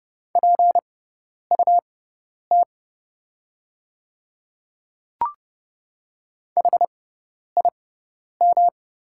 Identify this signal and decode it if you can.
Morse code, heard as a keyed tone, about 29 wpm, spaced as for 10 wpm, 700 Hz. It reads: PUT HIM
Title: 6x Character Spacing - Top 300 Words - 30wpm